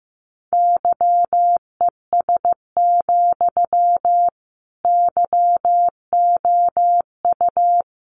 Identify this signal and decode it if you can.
Morse code, heard as a keyed tone, about 15 wpm, 700 Hz. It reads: YES, YOU
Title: Sentences from Top 500 Words 15wpm